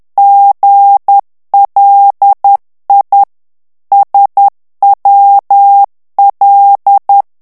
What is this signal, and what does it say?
Morse code, heard as a keyed tone, about 10 wpm, 800 Hz. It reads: GLI SWL
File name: ITALIEN texte clair 2 vitesse 10 mots minute